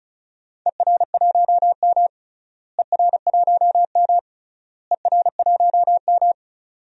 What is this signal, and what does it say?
Morse code, heard as a keyed tone, about 36 wpm, 700 Hz. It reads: ER1M ER1M ER1M
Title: DXCC-100 Call Signs - 4 Characters - Repeated 3x - 35wpm